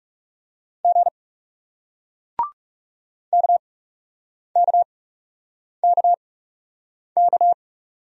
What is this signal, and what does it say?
Morse code, heard as a keyed tone, about 40 wpm, spaced as for 8 wpm, 700 Hz. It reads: G KKKK